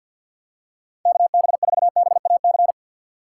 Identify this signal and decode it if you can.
Morse code, heard as a keyed tone, about 46 wpm, 700 Hz. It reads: KB4BAC
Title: USA Call Signs - 6 Characters - 4x Character Spacing - 50wpm